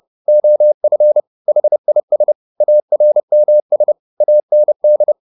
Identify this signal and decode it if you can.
Morse code, heard as a keyed tone, about 30 wpm, 600 Hz. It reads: OF HIS ARMS AND
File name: the old man and the sea / the old man and the sea 1 30wpm